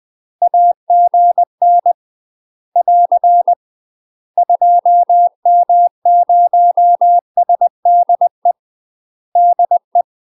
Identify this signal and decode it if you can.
Morse code, heard as a keyed tone, about 20 wpm, 700 Hz. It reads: AGN <AR> 2M0SDE DE
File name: UK MASTER QSOs TRAINING / 20WPM / UK MASTER QSOs 0002